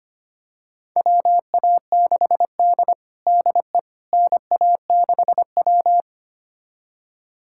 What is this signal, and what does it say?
Morse code, heard as a keyed tone, about 25 wpm, 700 Hz. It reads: WA6D DE NA6W
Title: USA Callsign DE Callsign - 4 Characters - 25wpm